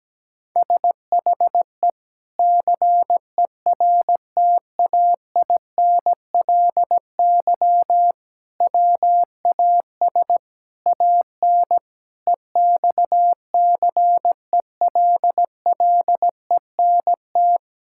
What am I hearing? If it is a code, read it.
Morse code, heard as a keyed tone, about 17 wpm, 700 Hz. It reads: SHE CERTAINLY WAS AN EXCELLENT
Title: Sentences from Top 1000 Words - Review Part 6 - 17wpm